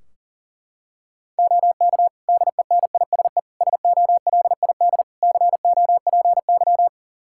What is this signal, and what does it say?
Morse code, heard as a keyed tone, about 40 wpm, 700 Hz. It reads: OK DENISE SOLID COPY
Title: Practice QSO - Exchange 3 - More Info - 40wpm